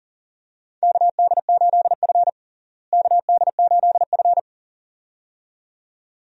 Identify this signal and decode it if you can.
Morse code, heard as a keyed tone, about 40 wpm, 700 Hz. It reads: KD8F KD8F
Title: USA Call Signs - 4 Characters - Repeated 2x - 40wpm